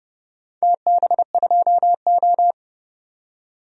Morse code, heard as a keyed tone, about 30 wpm, 700 Hz. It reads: TB2O
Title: DXCC-100 Call Signs - 4 Characters - 30wpm